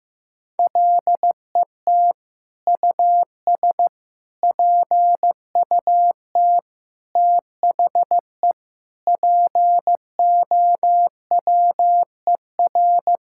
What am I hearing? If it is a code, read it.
Morse code, heard as a keyed tone, about 15 wpm, 700 Hz. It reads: LET US PUT THE POWER
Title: Sentences from Top 1000 Words - Review Part 5 - 15wpm